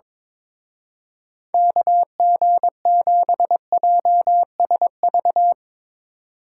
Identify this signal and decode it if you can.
Morse code, heard as a keyed tone, about 22 wpm, 700 Hz. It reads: KG7JSV
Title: Speed Racing - USA 2x3 Call Signs 22wpm